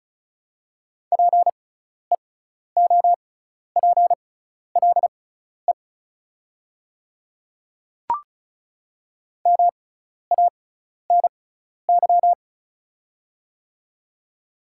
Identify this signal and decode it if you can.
Morse code, heard as a keyed tone, about 36 wpm, spaced as for 12 wpm, 690 Hz. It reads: PEOPLE MANY
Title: 6x Character Spacing - Top 100 Words - 35wpm